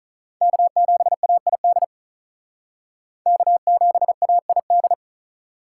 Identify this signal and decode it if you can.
Morse code, heard as a keyed tone, about 43 wpm, 700 Hz. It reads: K7AID K7AID
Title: Speed Racing - USA 1x3 Call Signs 35wpm